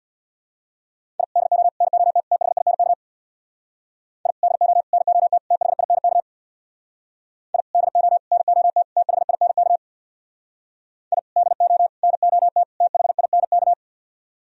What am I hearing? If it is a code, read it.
Morse code, heard as a keyed tone, about 63 wpm, 700 Hz. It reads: I DO NOT THINK I DO NOT THINK I DO NOT THINK I DO NOT THINK